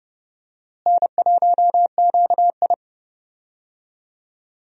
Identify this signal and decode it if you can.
Morse code, heard as a keyed tone, about 30 wpm, 700 Hz. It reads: N1QI
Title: USA Call Signs - 4 Characters - Repeated 3x - 30wpm